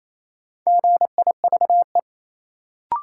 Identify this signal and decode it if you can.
Morse code, heard as a keyed tone, about 28 wpm, 700 Hz. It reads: GIVE
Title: Sentences from Top 200 Words - New Word Drill - 28wpm